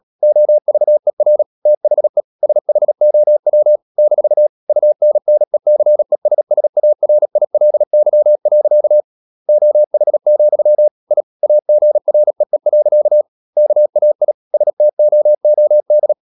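Morse code, heard as a keyed tone, about 37 wpm, 600 Hz. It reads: OVER THE SHOW - UNNECESSARILY. OH, I AGREE. KAI STOOD
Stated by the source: Morse code practice at 35–39 wpm sending English text of Ken Follett's novel Never, from CW med Gnister.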